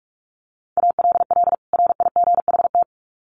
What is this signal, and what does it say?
Morse code, heard as a keyed tone, about 44 wpm, 700 Hz. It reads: ALL RIGHT
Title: Sets of 2 Words 45wpm